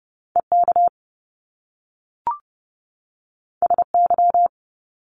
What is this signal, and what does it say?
Morse code, heard as a keyed tone, about 30 wpm, 700 Hz. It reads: EK SY
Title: Top Two Letter Combinations in Words 30wpm